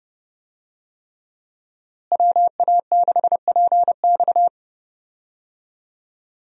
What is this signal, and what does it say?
Morse code, heard as a keyed tone, about 30 wpm, 700 Hz. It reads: WA6PX
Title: USA Call Signs - 5 Characters - Repeated 3x - 30wpm